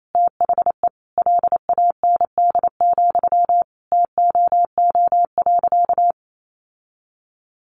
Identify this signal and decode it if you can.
Morse code, heard as a keyed tone, about 28 wpm, 700 Hz. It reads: THE LAND, TOO.
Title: Sentences from Top 1500 Words 28wpm